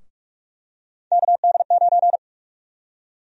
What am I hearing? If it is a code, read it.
Morse code, heard as a keyed tone, about 46 wpm, 700 Hz. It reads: KD9
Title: USA Call Sign Prefixes - Repeated 2x - 45wpm